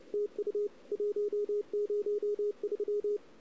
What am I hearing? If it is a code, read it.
Morse code, heard as a keyed tone, about 35 wpm, 410 Hz. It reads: TU 103
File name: CW / 2008-ss-cw-k5tr-op-n5rz -00:00